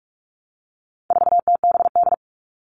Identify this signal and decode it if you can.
Morse code, heard as a keyed tone, about 44 wpm, 700 Hz. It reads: 4TBD
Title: All Cut Numbers - 4-Digit 45wpm